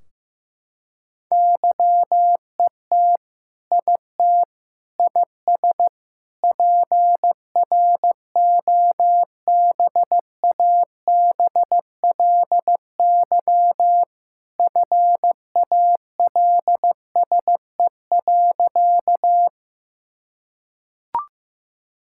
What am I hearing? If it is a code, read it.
Morse code, heard as a keyed tone, about 15 wpm, 700 Hz. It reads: YET IT IS PROBABLY FALSE.
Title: Sentences from Top 1600 Words 15wpm